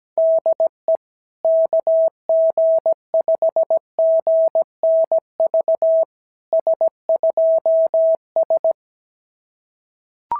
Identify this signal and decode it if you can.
Morse code, heard as a keyed tone, about 17 wpm, 650 Hz. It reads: DE KG5GNV S2S
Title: SOTA QSO Practice 17wpm